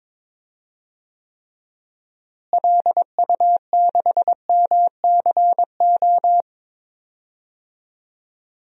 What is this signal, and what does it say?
Morse code, heard as a keyed tone, about 21 wpm, 700 Hz. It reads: LU6MCO
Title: DXCC-100 Call Signs - 6 Characters - Repeated 2x - 22wpm